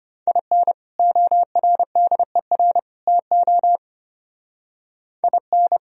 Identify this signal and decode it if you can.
Morse code, heard as a keyed tone, about 31 wpm, 700 Hz. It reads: IN ORDER TO IN
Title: Speed-Racing - Sets of 3 Words 25wpm